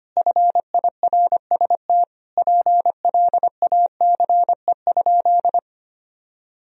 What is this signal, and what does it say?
Morse code, heard as a keyed tone, about 25 wpm, 700 Hz. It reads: FIRST PLACE?
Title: Sentences from Top 200 Words and Abbreviations 25wpm